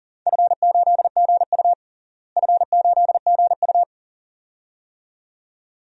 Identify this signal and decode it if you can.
Morse code, heard as a keyed tone, about 40 wpm, 700 Hz. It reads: F8GU F8GU
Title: DXCC-100 Call Signs - 4 Characters - Repeated 2x - 40wpm